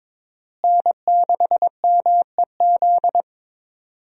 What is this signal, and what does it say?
Morse code, heard as a keyed tone, about 22 wpm, 700 Hz. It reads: N6MEZ
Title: USA Call Signs - 5 Characters - Repeated 2x - 22wpm